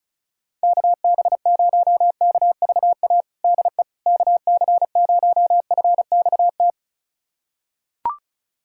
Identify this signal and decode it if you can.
Morse code, heard as a keyed tone, about 36 wpm, 700 Hz. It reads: KB0KVA DE KC0FXT
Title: USA Callsign DE Callsign - 6 Characters - 35wpm